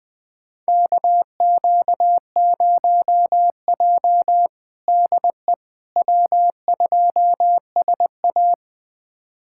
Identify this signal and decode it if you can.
Morse code, heard as a keyed tone, about 20 wpm, 700 Hz. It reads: KQ0J DE W2SA
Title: USA Callsign DE Callsign - 4 Characters - 20wpm